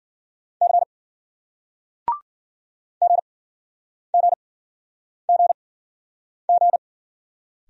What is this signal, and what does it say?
Morse code, heard as a keyed tone, about 49 wpm, spaced as for 9 wpm, 700 Hz. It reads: C GGGG
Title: Speed-Racing Single Letters 40wpm